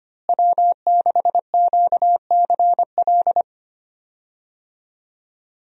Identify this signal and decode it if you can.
Morse code, heard as a keyed tone, about 25 wpm, 700 Hz. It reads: W6QCL
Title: USA Call Signs - 5 Characters - Repeated 3x - 25wpm